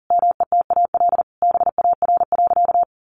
Morse code, heard as a keyed tone, about 40 wpm, 700 Hz. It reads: METAL BAR.
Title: Additional Easy Phrases 40wpm